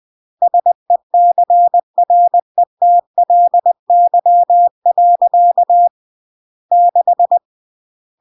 Morse code, heard as a keyed tone, about 20 wpm, 700 Hz. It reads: SECRETLY. 6